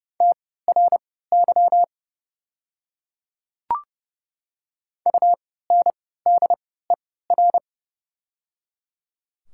Morse code, heard as a keyed tone, about 30 wpm, spaced as for 17 wpm, 700 Hz. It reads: TRY UNDER
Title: 3x Character Spacing - Top 200 Words - 30wpm